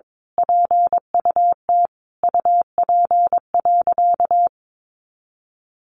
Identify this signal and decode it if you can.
Morse code, heard as a keyed tone, about 22 wpm, 700 Hz. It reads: PUT UP.
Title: A Sound Of Thunder 22wpm